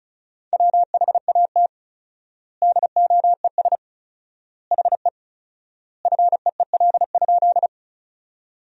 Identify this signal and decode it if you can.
Morse code, heard as a keyed tone, about 36 wpm, 700 Hz. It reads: WHAT DOES HE FEEL?
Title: 4x Word-Spacing - Sentences from Top 400 Words - 35wpm